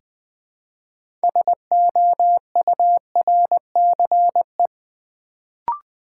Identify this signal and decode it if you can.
Morse code, heard as a keyed tone, about 20 wpm, 700 Hz. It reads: SOURCE
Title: Build A Word - 2 Letters At A Time - 6-Letter Words - 20wpm